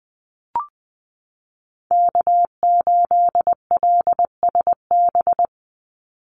Morse code, heard as a keyed tone, about 20 wpm, 700 Hz. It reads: K8LSB
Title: Call Signs Repeated 3x - 20wpm